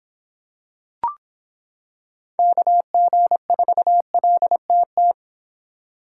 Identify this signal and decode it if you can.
Morse code, heard as a keyed tone, about 26 wpm, 700 Hz. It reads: KG4LTT